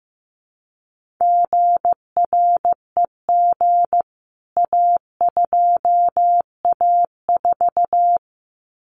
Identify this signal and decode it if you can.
Morse code, heard as a keyed tone, about 15 wpm, 700 Hz. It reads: GREG A2A4
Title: CWT Exchange 15wpm